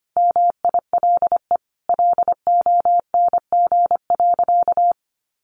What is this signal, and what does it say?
Morse code, heard as a keyed tone, about 25 wpm, 700 Hz. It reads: MILE LONG.
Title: Sentences from Top 1300 Words 25wpm